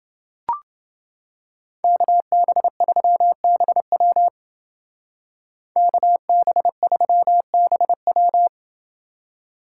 Morse code, heard as a keyed tone, about 28 wpm, 700 Hz. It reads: KB3BW KB3BW